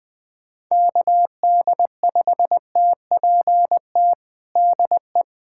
Morse code, heard as a keyed tone, about 21 wpm, 700 Hz. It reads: KD5TPT DE
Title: USA Callsign DE Callsign 20wpm